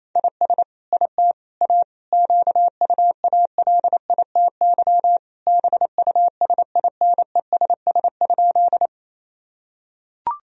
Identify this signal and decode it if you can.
Morse code, heard as a keyed tone, about 28 wpm, 700 Hz. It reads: IS IT A QUALITY BUSINESS?
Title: Sentences from Top 1000 Words - Review Part 4 - 28wpm